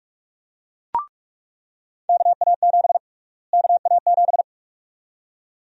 Morse code, heard as a keyed tone, about 47 wpm, 700 Hz. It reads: KA7 KA7
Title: USA Call Sign Prefixes - Repeated 2x - 45wpm